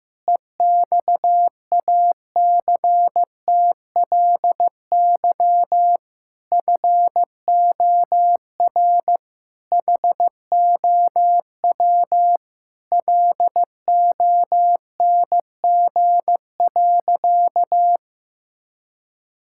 Morse code, heard as a keyed tone, about 15 wpm, 700 Hz. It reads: EXACTLY FOR HOW LONG.
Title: Sentences from Top 1000 Words - Review Part 6 - 15wpm